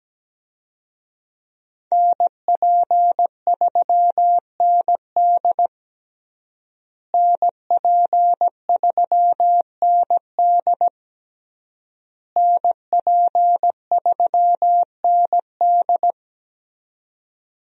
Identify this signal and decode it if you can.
Morse code, heard as a keyed tone, about 17 wpm, 700 Hz. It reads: NP3ND NP3ND NP3ND
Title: USA Call Signs - 5 Characters - Repeated 3x - 17wpm